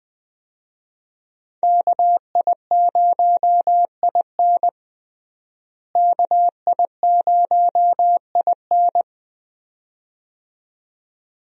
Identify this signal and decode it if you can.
Morse code, heard as a keyed tone, about 20 wpm, 700 Hz. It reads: KI0IN KI0IN
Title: USA Call Signs - 5 Characters - Repeated 2x - 20wpm